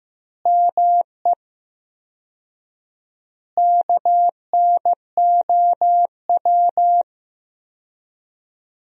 Morse code, heard as a keyed tone, about 15 wpm, 700 Hz. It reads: ME KNOW
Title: 4x Word-Spacing - Sentences from Top 300 Words - 15wpm